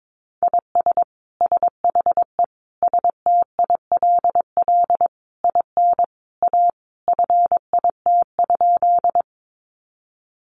Morse code, heard as a keyed tone, about 22 wpm, 700 Hz. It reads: IS SHE STILL IN A FIT?
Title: Sentences from Top 800 Words 22wpm